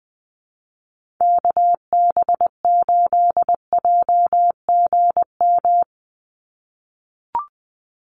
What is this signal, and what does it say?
Morse code, heard as a keyed tone, about 20 wpm, 700 Hz. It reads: KB8JGM
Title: USA Call Signs 20wpm